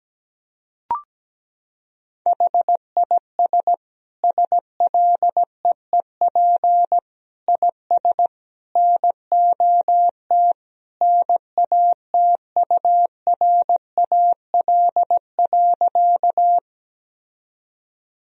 Morse code, heard as a keyed tone, about 17 wpm, 700 Hz. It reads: HIS SLEEP IS NOT NATURAL.
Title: Sentences from Top 1000 Words - Review Part 4 - 17wpm